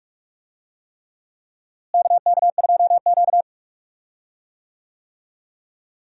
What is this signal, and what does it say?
Morse code, heard as a keyed tone, about 45 wpm, 700 Hz. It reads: KK2Q
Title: USA Call Signs - 4 Characters - Repeated 3x - 45wpm